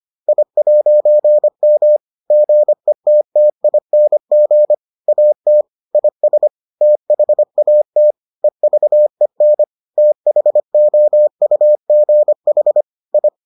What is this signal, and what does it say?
Morse code, heard as a keyed tone, about 25 wpm, 600 Hz. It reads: I'M GETTING AT IS THAT EVEN THOUGH I